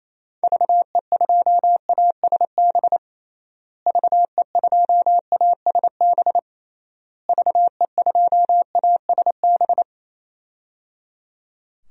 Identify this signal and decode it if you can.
Morse code, heard as a keyed tone, about 28 wpm, 700 Hz. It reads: VE2ASB VE2ASB VE2ASB
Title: DXCC-100 Call Signs - 6 Characters - Repeated 3x - 28wpm